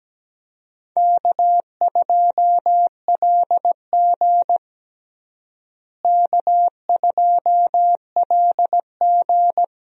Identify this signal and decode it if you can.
Morse code, heard as a keyed tone, about 17 wpm, 700 Hz. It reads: K2LG K2LG
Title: USA Call Signs - 4 Characters - Repeated 2x - 17wpm